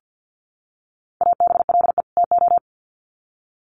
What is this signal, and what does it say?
Morse code, heard as a keyed tone, about 50 wpm, 700 Hz. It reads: ABLE TO
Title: Sets of 2 Words 50wpm